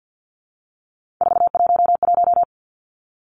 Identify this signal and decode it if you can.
Morse code, heard as a keyed tone, about 50 wpm, 700 Hz. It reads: 411
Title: Three Challenging Letter-Number Combinations 50wpm